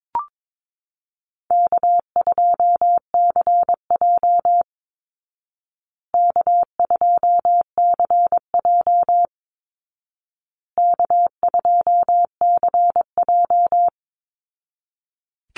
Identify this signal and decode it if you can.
Morse code, heard as a keyed tone, about 22 wpm, 700 Hz. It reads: K2CJ K2CJ K2CJ
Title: Easy Call Signs Repeated 3x - 22wpm